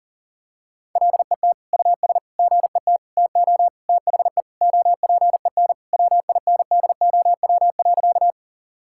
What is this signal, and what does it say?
Morse code, heard as a keyed tone, about 37 wpm, 700 Hz. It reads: LET US GET TO THE OPEN WINDOW.
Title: Sentences from Top 1600 Words 40wpm